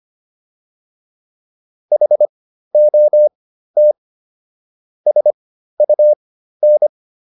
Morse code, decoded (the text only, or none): HOT SUN